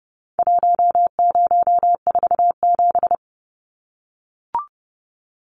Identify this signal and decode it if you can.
Morse code, heard as a keyed tone, about 30 wpm, 700 Hz. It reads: 1047
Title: Numbers - Contest Style - 4-Digit 30wpm